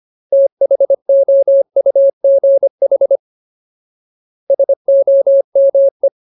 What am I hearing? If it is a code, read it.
Morse code, heard as a keyed tone, about 25 wpm, 550 Hz. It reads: THOUGH SOME
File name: Ultimate Word List / 25WPM Ultimate